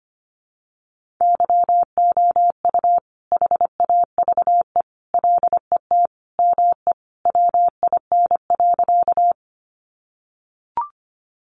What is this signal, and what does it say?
Morse code, heard as a keyed tone, about 25 wpm, 700 Hz. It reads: YOU HAVE LET ME WIN.
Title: Sentences from Top 1500 Words 25wpm